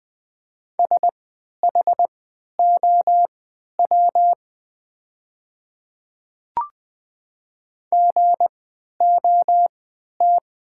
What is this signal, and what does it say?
Morse code, heard as a keyed tone, about 20 wpm, spaced as for 11 wpm, 700 Hz. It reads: SHOW GOT
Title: 3x Character Spacing - Top 500 Words - 20wpm